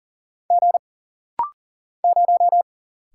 Morse code, heard as a keyed tone, about 39 wpm, 700 Hz. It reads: G 0
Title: Single Letter-Number Alt - Rapid-Fire 40wpm